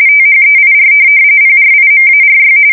RTTY, about 45.45 baud. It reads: THEIR PRIMARY
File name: fsk170 / USS-New-Jersey